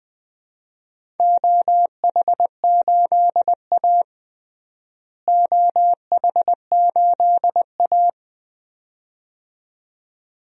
Morse code, heard as a keyed tone, about 20 wpm, 700 Hz. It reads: OH8A OH8A
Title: DXCC-100 Call Signs - 4 Characters - Repeated 2x - 20wpm